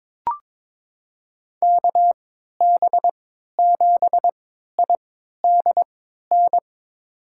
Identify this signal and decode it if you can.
Morse code, heard as a keyed tone, about 22 wpm, spaced as for 12 wpm, 700 Hz. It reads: KB7IDN
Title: USA Call Signs - 6 Characters - 3x Character Spacing - 22wpm